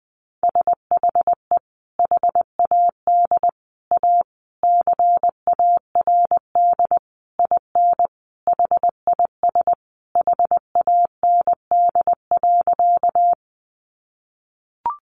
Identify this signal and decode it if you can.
Morse code, heard as a keyed tone, about 20 wpm, 700 Hz. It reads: SHE HAD A CARD IN HIS HAND.
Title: Sentences from Top 600 Words 20wpm